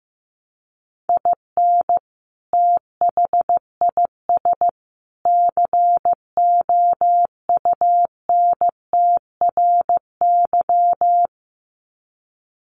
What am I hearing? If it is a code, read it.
Morse code, heard as a keyed tone, about 15 wpm, 700 Hz. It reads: IN THIS COUNTRY
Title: Sets of 3 Words 15wpm